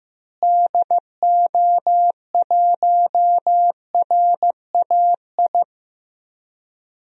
Morse code, heard as a keyed tone, about 16 wpm, 700 Hz. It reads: DO1RAI
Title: DXCC-100 Call Signs - 6 Characters - Repeated 2x - 15wpm